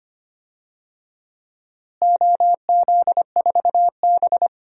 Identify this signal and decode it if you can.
Morse code, heard as a keyed tone, about 25 wpm, 700 Hz. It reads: OZ4B
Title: DXCC-100 Call Signs - 4 Characters - Repeated 2x - 25wpm